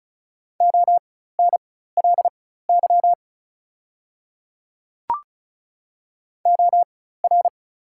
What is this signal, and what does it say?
Morse code, heard as a keyed tone, about 36 wpm, spaced as for 16 wpm, 700 Hz. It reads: ONLY OR
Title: 4x Character Spacing - Top 300 Words - 35wpm